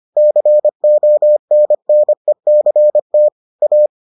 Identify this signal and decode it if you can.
Morse code, heard as a keyed tone, about 25 wpm, 600 Hz. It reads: CONNECT A